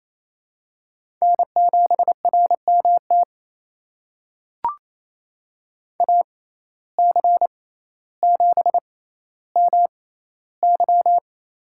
Morse code, heard as a keyed tone, about 28 wpm, spaced as for 14 wpm, 700 Hz. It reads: N7RMT AC7MY
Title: USA Call Signs - 5 Characters - 6x Character Spacing - 28wpm